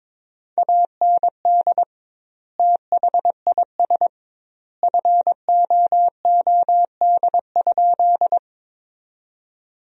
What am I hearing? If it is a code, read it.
Morse code, heard as a keyed tone, about 22 wpm, 700 Hz. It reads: AND THIS FOOD?
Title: 2x Word-Spacing - Sentences from Top 300 Words - 22wpm